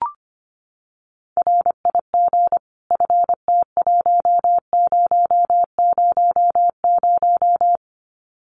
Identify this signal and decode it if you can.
Morse code, heard as a keyed tone, about 25 wpm, 700 Hz. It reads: RIG FT1000
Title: Common QSO Phrases 25wpm